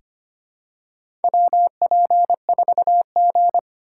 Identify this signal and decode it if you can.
Morse code, heard as a keyed tone, about 25 wpm, 700 Hz. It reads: WP4G
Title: USA Call Signs - 4 Characters - 2x Character Spacing - 25wpm